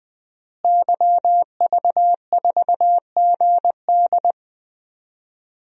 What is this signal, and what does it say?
Morse code, heard as a keyed tone, about 20 wpm, 700 Hz. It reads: YV4GD
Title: DXCC-100 Call Signs - 5 Characters - Repeated 2x - 20wpm